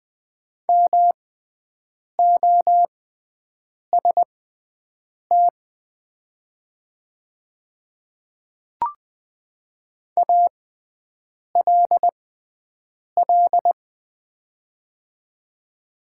Morse code, heard as a keyed tone, about 20 wpm, spaced as for 7 wpm, 700 Hz. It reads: MOST ALL
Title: 6x Character Spacing - Top 100 Words - 20wpm